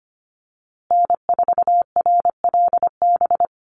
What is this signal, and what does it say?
Morse code, heard as a keyed tone, about 25 wpm, 700 Hz. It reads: N4RLB
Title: USA Call Signs Repeated 2x 25wpm